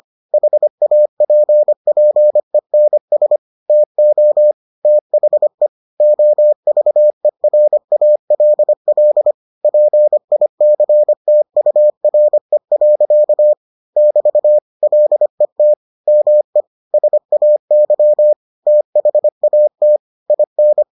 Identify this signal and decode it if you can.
Morse code, heard as a keyed tone, about 25 wpm, 600 Hz. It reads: HAPPENS TO THE OVERALL PICTURE. <BT> LET ME SAY THAT IN